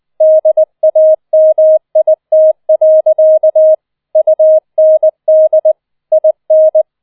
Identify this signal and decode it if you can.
Morse code, heard as a keyed tone, about 20 wpm, 630 Hz. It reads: DAMIT. UND IN